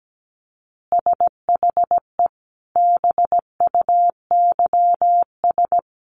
Morse code, heard as a keyed tone, about 17 wpm, 700 Hz. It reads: SHE BUYS
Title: Easy Phrases 17wpm